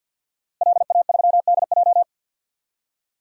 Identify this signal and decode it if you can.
Morse code, heard as a keyed tone, about 53 wpm, 700 Hz. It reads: LA3DJ
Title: DXCC-100 Call Signs - 5 Characters - Repeated 2x - 50wpm